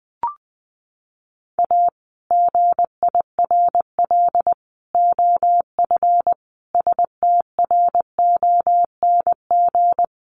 Morse code, heard as a keyed tone, about 20 wpm, 700 Hz. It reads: A GIRL OF STRONG
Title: Sentences from Top 1200 Words 20wpm